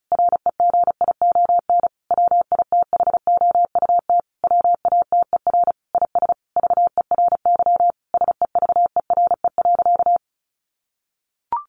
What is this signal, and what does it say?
Morse code, heard as a keyed tone, about 34 wpm, 700 Hz. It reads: REGION WITHOUT WATER IS VERY SEVERE.
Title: Sentences from Top 1500 Words 35wpm